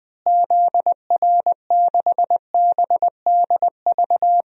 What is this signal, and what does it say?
Morse code, heard as a keyed tone, about 21 wpm, 700 Hz. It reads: ZR6BDV
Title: DXCC-100 Call Signs - 6 Characters - Repeated 2x - 20wpm